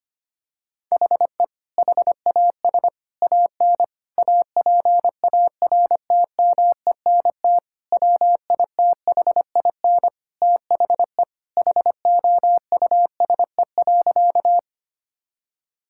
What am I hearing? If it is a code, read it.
Morse code, heard as a keyed tone, about 25 wpm, 700 Hz. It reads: HE HAS AN APARTMENT WITHIN THE HOUSE.
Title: Sentences from Top 1000 Words - Review Part 6 - 25wpm